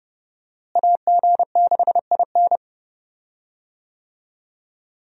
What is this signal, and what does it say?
Morse code, heard as a keyed tone, about 30 wpm, 700 Hz. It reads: AG6IN
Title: USA Call Signs - 5 Characters - Repeated 3x - 30wpm